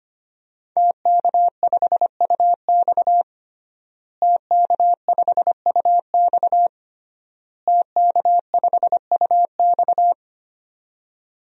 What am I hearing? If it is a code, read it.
Morse code, heard as a keyed tone, about 25 wpm, 700 Hz. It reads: TK5UX TK5UX TK5UX
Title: DXCC-100 Call Signs - 5 Characters - Repeated 3x - 25wpm